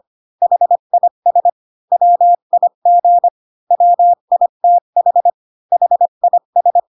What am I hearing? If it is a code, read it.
Morse code, heard as a keyed tone, about 25 wpm, 700 Hz. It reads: HIS WIG WITH HIS
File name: Pinocchio / Pinocchio ch1 20 25 ik1hge